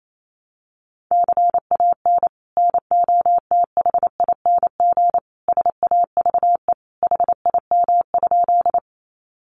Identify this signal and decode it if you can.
Morse code, heard as a keyed tone, about 28 wpm, 700 Hz. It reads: CAN NOTHING SAVE HIM?
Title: Sentences from Top 1100 Words 28wpm